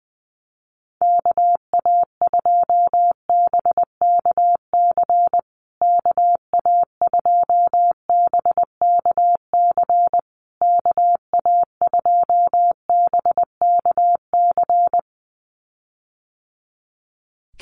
Morse code, heard as a keyed tone, about 20 wpm, 700 Hz. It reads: KA2BKC KA2BKC KA2BKC
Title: USA Call Signs Repeated 3x 20wpm